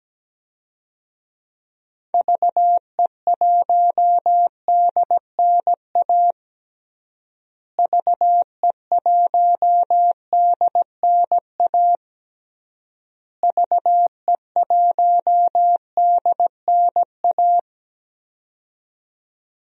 Morse code, heard as a keyed tone, about 17 wpm, 700 Hz. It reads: VE1DNA VE1DNA VE1DNA
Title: DXCC-100 Call Signs - 6 Characters - Repeated 3x - 17wpm